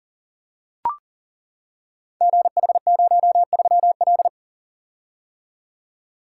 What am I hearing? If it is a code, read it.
Morse code, heard as a keyed tone, about 42 wpm, 690 Hz. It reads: GH03L
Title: Code Groups - 5 Letter:Numbers - 40wpm